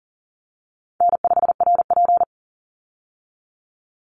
Morse code, heard as a keyed tone, about 40 wpm, 700 Hz. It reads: N5RP
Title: Call Signs - 40wpm